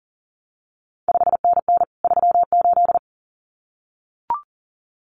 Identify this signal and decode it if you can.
Morse code, heard as a keyed tone, about 40 wpm, 700 Hz. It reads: 5NN 38
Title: CQ WW DX Contest Exchange Encore - 40wpm